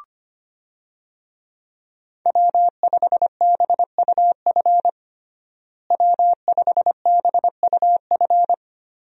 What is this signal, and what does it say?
Morse code, heard as a keyed tone, about 25 wpm, 700 Hz. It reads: W5BUF W5BUF
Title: USA Call Signs - 5 Characters - Repeated 3x - 25wpm